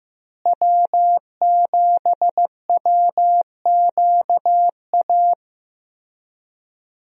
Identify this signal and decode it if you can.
Morse code, heard as a keyed tone, about 15 wpm, 700 Hz. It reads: W7WQA
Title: USA Call Signs - 5 Characters - Repeated 3x - 15wpm